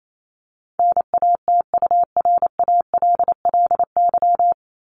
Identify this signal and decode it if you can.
Morse code, heard as a keyed tone, about 28 wpm, 700 Hz. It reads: NATURALLY
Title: Top 1001-1250 Words 28wpm